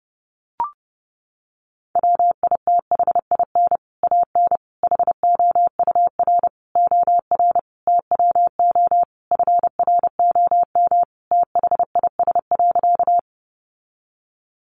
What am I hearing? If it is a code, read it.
Morse code, heard as a keyed tone, about 30 wpm, 700 Hz. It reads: WITHIN AN HOUR OR TWO FROM THIS.
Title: Sentences from Top 700 Words 30wpm